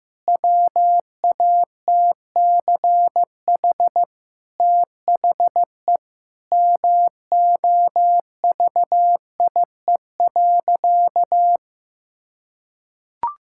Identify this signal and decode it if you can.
Morse code, heard as a keyed tone, about 15 wpm, 700 Hz. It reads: WATCH THE MOVIE.
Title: Sentences from Top 1000 Words - Review Part 5 - 15wpm